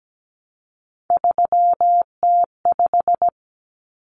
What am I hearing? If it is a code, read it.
Morse code, heard as a keyed tone, about 17 wpm, 700 Hz. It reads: 3T5
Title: Common Cut Numbers - 3-Digit 17wpm